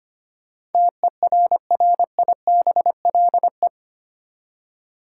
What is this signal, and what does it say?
Morse code, heard as a keyed tone, about 25 wpm, 700 Hz. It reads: TERRIBLE
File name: 8 Letter Words - First 3 Letters Given - 25wpm